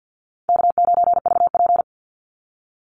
Morse code, heard as a keyed tone, about 50 wpm, 700 Hz. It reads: K9VP
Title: Call Signs - 50wpm